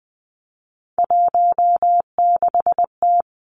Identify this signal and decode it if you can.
Morse code, heard as a keyed tone, about 20 wpm, 700 Hz. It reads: 16T
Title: Three Challenging Letter-Number Combinations 20wpm